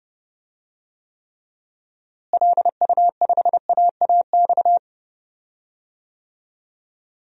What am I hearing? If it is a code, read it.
Morse code, heard as a keyed tone, about 32 wpm, 700 Hz. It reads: LU5AAX
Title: DXCC-100 Call Signs - 6 Characters - Repeated 3x - 30wpm